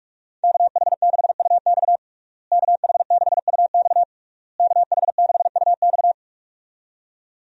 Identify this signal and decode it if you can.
Morse code, heard as a keyed tone, about 42 wpm, 700 Hz. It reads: KH6UX KH6UX KH6UX
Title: USA Call Signs - 5 Characters - Repeated 3x - 45wpm